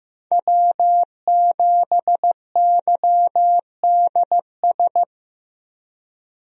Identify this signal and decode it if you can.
Morse code, heard as a keyed tone, about 15 wpm, 700 Hz. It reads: W7YDS
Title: USA Call Signs - 5 Characters - Repeated 3x - 15wpm